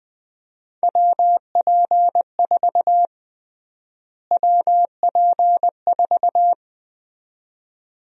Morse code, heard as a keyed tone, about 20 wpm, 700 Hz. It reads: WP4 WP4
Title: USA Call Sign Prefixes - Repeated 2x - 20wpm